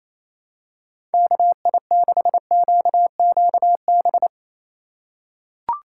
Morse code, heard as a keyed tone, about 27 wpm, 700 Hz. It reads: KI6QQB
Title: USA Call Signs - 6 Characters - 2x Character Spacing - 28wpm